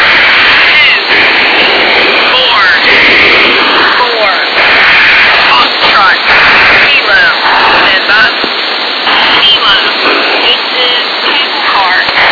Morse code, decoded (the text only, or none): none